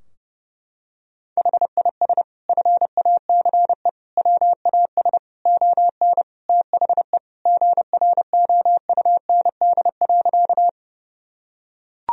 Morse code, heard as a keyed tone, about 30 wpm, 700 Hz. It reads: HIS FACE WAS ON THE GROUND.
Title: Sentences from Top 1000 Words - Review Part 2 - 30wpm